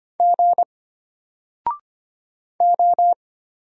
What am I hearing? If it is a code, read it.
Morse code, heard as a keyed tone, about 25 wpm, 700 Hz. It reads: G O
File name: Single-Letter-Number-25wpm